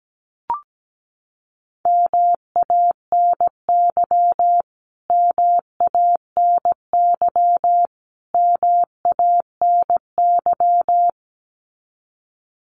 Morse code, heard as a keyed tone, about 17 wpm, 700 Hz. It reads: MANY MANY MANY
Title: Top 500 Words Repeated 17wpm